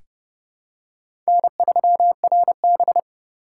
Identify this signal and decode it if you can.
Morse code, heard as a keyed tone, about 32 wpm, 700 Hz. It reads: N3RB
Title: USA Call Signs - 4 Characters - 2x Character Spacing - 30wpm